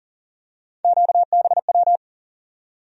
Morse code, heard as a keyed tone, about 40 wpm, 700 Hz. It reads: QBW
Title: USA Call Sign Postfixes - 3 Letters 40wpm